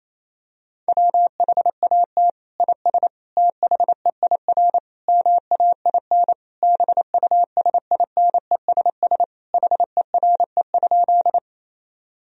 Morse code, heard as a keyed tone, about 27 wpm, 700 Hz. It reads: WHAT IS THEIR MAIN BUSINESS HERE?
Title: Sentences from Top 1000 Words - Review Part 2 - 28wpm